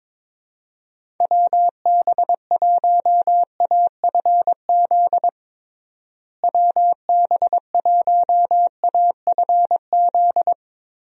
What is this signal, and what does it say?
Morse code, heard as a keyed tone, about 22 wpm, 700 Hz. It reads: WB1AFZ WB1AFZ
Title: USA Call Signs - 6 Characters - Repeated 3x - 22wpm